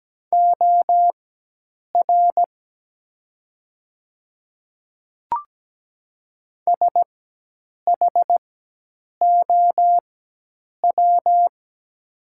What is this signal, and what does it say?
Morse code, heard as a keyed tone, about 17 wpm, spaced as for 8 wpm, 700 Hz. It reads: OR SHOW
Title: 4x Character Spacing - Top 300 Words - 17wpm